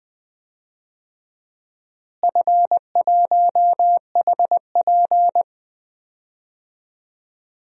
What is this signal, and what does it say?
Morse code, heard as a keyed tone, about 21 wpm, 700 Hz. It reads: F1HP
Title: DXCC-100 Call Signs - 4 Characters - Repeated 3x - 20wpm